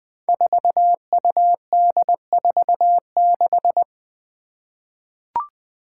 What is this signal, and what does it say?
Morse code, heard as a keyed tone, about 20 wpm, 700 Hz. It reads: 4UD46
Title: Five Challenging Letter-Number Combinations 20wpm